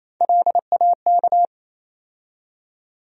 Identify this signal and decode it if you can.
Morse code, heard as a keyed tone, about 28 wpm, 700 Hz. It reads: LAK
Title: USA Call Sign Postfixes - 3 Letters 28wpm